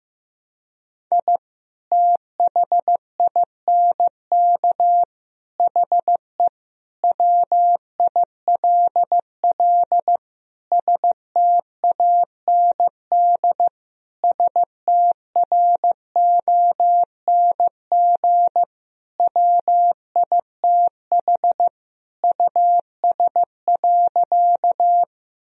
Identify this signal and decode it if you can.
Morse code, heard as a keyed tone, about 15 wpm, 700 Hz. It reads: I THINK HE WILL STAND STRONG WITH US.
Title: Sentences from Top 1000 Words - Review Part 3 - 15wpm